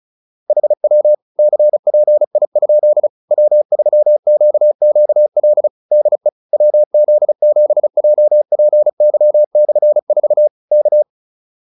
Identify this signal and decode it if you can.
Morse code, heard as a keyed tone, about 36 wpm, 600 Hz. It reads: HW CPI? W3QQL DE WZ7JPY/4 K